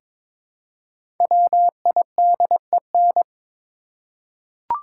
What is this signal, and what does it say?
Morse code, heard as a keyed tone, about 22 wpm, 700 Hz. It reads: WIDEN
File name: 5 Letter Words - First 3 Letters Given - 22wpm